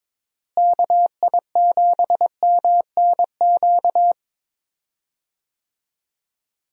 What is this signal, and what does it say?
Morse code, heard as a keyed tone, about 22 wpm, 700 Hz. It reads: KI7MNQ
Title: USA Call Signs - 6 Characters - Repeated 2x - 22wpm